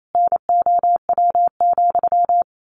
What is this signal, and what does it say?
Morse code, heard as a keyed tone, about 28 wpm, 700 Hz. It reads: NOW,
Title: A Sound Of Thunder 28wpm